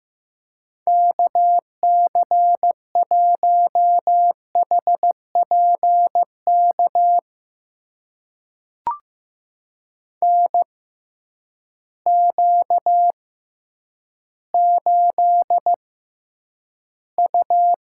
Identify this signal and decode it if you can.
Morse code, heard as a keyed tone, about 15 wpm, spaced as for 9 wpm, 700 Hz. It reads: KC1HPK NQ8U
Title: USA Call Signs - 6x Character Spacing - 15wpm